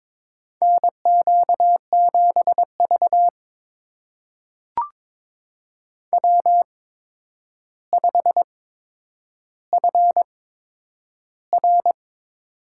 Morse code, heard as a keyed tone, about 22 wpm, spaced as for 9 wpm, 700 Hz. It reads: NQ7V W5FR
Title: USA Call Signs - 4 Characters - 8x Character Spacing - 22wpm